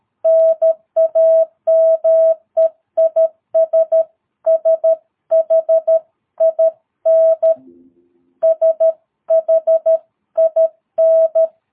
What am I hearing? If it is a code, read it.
Morse code, heard as a keyed tone, about 13 wpm, 660 Hz. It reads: NAME IS SHIN SHIN
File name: jh8cbh / 2022cw